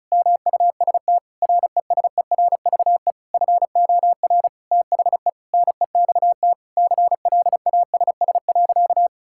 Morse code, heard as a keyed tone, about 36 wpm, 700 Hz. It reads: MUST RESERVE FOR THE NEXT CLASS.
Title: Sentences from Top 1000 Words - Review Part 6 - 35wpm